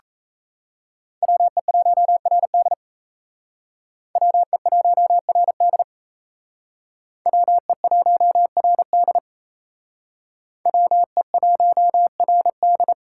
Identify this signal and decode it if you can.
Morse code, heard as a keyed tone, about 33 wpm, 700 Hz. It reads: WE1RD WE1RD WE1RD WE1RD
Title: Speed Racing - USA 2x2 Call Signs 28wpm